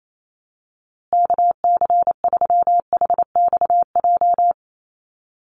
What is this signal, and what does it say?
Morse code, heard as a keyed tone, about 28 wpm, 700 Hz. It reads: KC3HXJ
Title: USA Call Signs Repeated 3x 28wpm